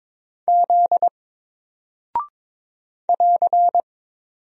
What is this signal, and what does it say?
Morse code, heard as a keyed tone, about 22 wpm, 700 Hz. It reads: Z <AR>
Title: Single Character (Letter:Number:Punctuation:Prosign) 22wpm